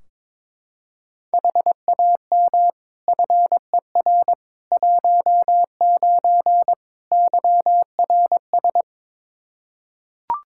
Morse code, heard as a keyed tone, about 22 wpm, 700 Hz. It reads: HAM FER 19 YRS
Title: Practice QSO - Exchange 3 - More Info - 22wpm